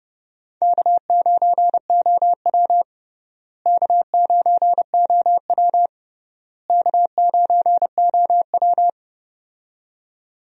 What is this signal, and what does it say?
Morse code, heard as a keyed tone, about 30 wpm, 700 Hz. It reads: K9OW K9OW K9OW
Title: USA Call Signs - 4 Characters - Repeated 3x - 30wpm